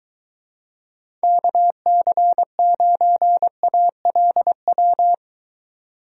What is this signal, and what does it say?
Morse code, heard as a keyed tone, about 22 wpm, 700 Hz. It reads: KC9ALW